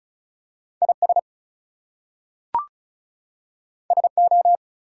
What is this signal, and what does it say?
Morse code, heard as a keyed tone, about 36 wpm, 700 Hz. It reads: IS SO
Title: Two Letter Words - 35wpm